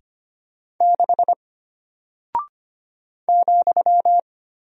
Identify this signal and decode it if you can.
Morse code, heard as a keyed tone, about 25 wpm, 700 Hz. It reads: 6 ,
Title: Single Character (Letter:Number:Punctuation:Prosign) Alt 25wpm